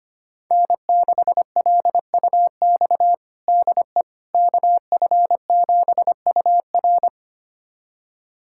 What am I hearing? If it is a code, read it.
Morse code, heard as a keyed tone, about 25 wpm, 700 Hz. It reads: N6LUX DE KF7UR
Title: USA Callsign DE Callsign - 5 Characters - 25wpm